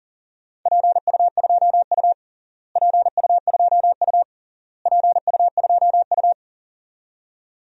Morse code, heard as a keyed tone, about 40 wpm, 700 Hz. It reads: PU2U PU2U PU2U
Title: DXCC-100 Call Signs - 4 Characters - Repeated 3x - 40wpm